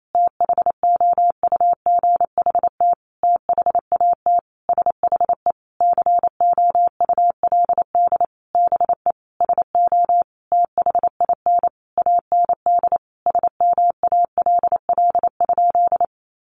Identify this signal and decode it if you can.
Morse code, heard as a keyed tone, about 28 wpm, 700 Hz. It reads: THOUGHT THAT SHE COULD BE SO THIN AND SMALL?
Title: Sentences from Top 1200 Words 28wpm